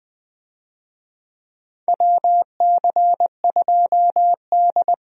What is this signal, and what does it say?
Morse code, heard as a keyed tone, about 20 wpm, 700 Hz. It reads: WC2D
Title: USA Call Signs - 4 Characters - Repeated 3x - 20wpm